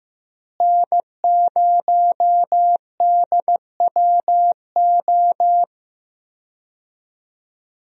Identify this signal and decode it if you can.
Morse code, heard as a keyed tone, about 15 wpm, 700 Hz. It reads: N0DWO
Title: USA Call Signs - 5 Characters - Repeated 2x - 15wpm